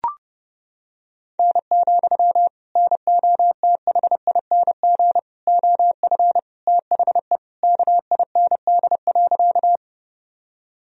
Morse code, heard as a keyed tone, about 30 wpm, 700 Hz. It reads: N, NOTHING OF THE KIND.
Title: Sentences from Top 500 Words and Abbreviations 30wpm